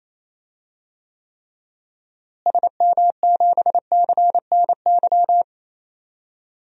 Morse code, heard as a keyed tone, about 28 wpm, 700 Hz. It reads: SM7CNY
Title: DXCC-100 Call Signs - 6 Characters - Repeated 3x - 28wpm